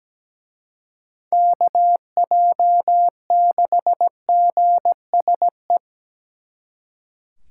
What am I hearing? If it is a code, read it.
Morse code, heard as a keyed tone, about 20 wpm, 700 Hz. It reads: KJ6GSE
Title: Speed Racing - USA 2x3 Call Signs 17wpm